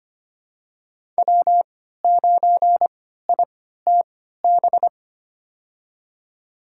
Morse code, heard as a keyed tone, about 25 wpm, spaced as for 14 wpm, 700 Hz. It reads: W9ITB
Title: USA Call Signs - 5 Characters - 3x Character Spacing - 25wpm